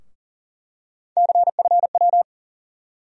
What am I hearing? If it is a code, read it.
Morse code, heard as a keyed tone, about 40 wpm, 700 Hz. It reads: CFW
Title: USA Call Sign Postfixes - 3 Letters 40wpm